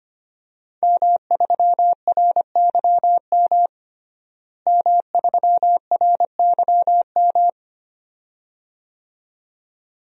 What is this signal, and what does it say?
Morse code, heard as a keyed tone, about 25 wpm, 700 Hz. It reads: M3RYM M3RYM
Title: DXCC-100 Call Signs - 5 Characters - Repeated 2x - 25wpm